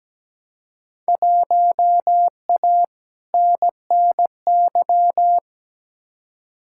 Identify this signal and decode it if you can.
Morse code, heard as a keyed tone, about 17 wpm, 700 Hz. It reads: 1A NNY
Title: Field Day Exchanges - 17wpm